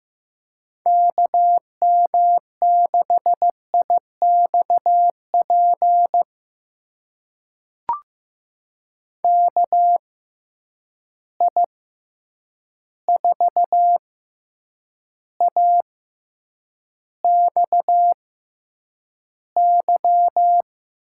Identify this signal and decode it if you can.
Morse code, heard as a keyed tone, about 15 wpm, spaced as for 8 wpm, 700 Hz. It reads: KM6IXP KI4AXY
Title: USA Call Signs - 6 Characters - 6x Character Spacing - 15wpm